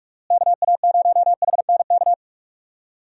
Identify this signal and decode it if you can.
Morse code, heard as a keyed tone, about 46 wpm, 700 Hz. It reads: KA0HNK
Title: USA Call Signs - 6 Characters - Repeated 3x - 45wpm